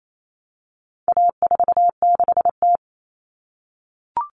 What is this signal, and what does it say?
Morse code, heard as a keyed tone, about 28 wpm, 700 Hz. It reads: A46T
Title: Common Cut Numbers - 4-Digit 28wpm